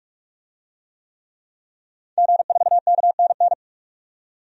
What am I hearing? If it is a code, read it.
Morse code, heard as a keyed tone, about 46 wpm, 700 Hz. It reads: G4KNN
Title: DXCC-100 Call Signs - 5 Characters - 45wpm